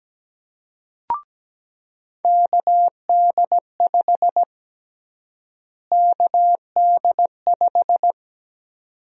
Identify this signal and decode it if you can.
Morse code, heard as a keyed tone, about 17 wpm, 700 Hz. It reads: KD5 KD5
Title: USA Call Sign Prefixes - Repeated 2x - 17wpm